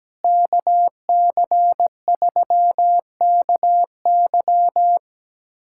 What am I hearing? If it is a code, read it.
Morse code, heard as a keyed tone, about 17 wpm, 700 Hz. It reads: KC3KY
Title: USA Callsign DE Callsign 17wpm